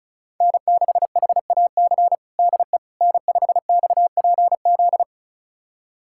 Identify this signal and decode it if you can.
Morse code, heard as a keyed tone, about 34 wpm, 700 Hz. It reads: N6HAC DE N5XPZ
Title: USA Callsign DE Callsign - 5 Characters - 35wpm